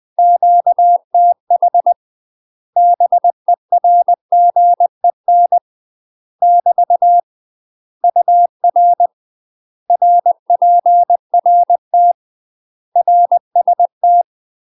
Morse code, heard as a keyed tone, about 20 wpm, 700 Hz. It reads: QTH BERGEN <BT> UR RPRT RST
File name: UK MASTER QSOs TRAINING / 20WPM / UK MASTER QSOs 0001